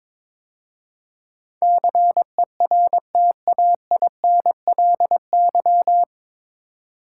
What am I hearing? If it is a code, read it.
Morse code, heard as a keyed tone, about 22 wpm, 700 Hz. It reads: CERTAINLY
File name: 9 Letter Words - First 3 Letters Given - 22wpm